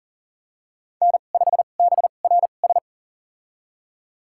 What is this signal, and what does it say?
Morse code, heard as a keyed tone, about 39 wpm, spaced as for 29 wpm, 700 Hz. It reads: N5BRS
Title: USA Call Signs - 5 Characters - 2x Character Spacing - 40wpm